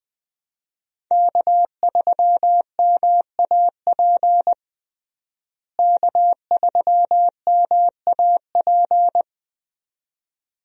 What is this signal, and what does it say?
Morse code, heard as a keyed tone, about 21 wpm, 700 Hz. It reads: K3MAP K3MAP
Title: USA Call Signs - 5 Characters - Repeated 2x - 20wpm